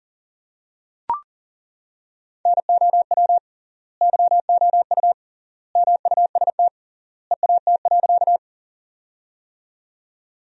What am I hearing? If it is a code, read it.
Morse code, heard as a keyed tone, about 40 wpm, 700 Hz. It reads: NOW YOU MUST EAT.
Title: 3x Word-Spacing - Sentences from Top 300 Words - 40wpm